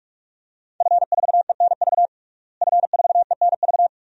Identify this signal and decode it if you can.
Morse code, heard as a keyed tone, about 46 wpm, 700 Hz. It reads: F4ENV F4ENV
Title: DXCC-100 Call Signs - 5 Characters - Repeated 2x - 45wpm